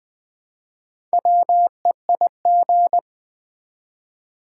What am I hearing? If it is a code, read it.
Morse code, heard as a keyed tone, about 20 wpm, 700 Hz. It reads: WEIG
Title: Build A Word - 1 Letter At A Time - 5-Letter Words - 20wpm